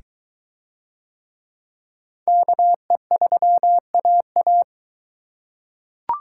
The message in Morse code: KE3AA

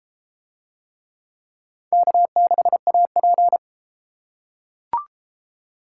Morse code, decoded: K6AP

